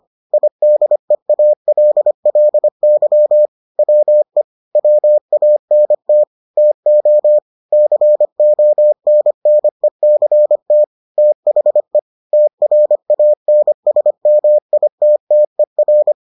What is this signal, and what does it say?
Morse code, heard as a keyed tone, about 25 wpm, 600 Hz. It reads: IDEALLY WE WANT TO CONNECT THE TRANSMITTER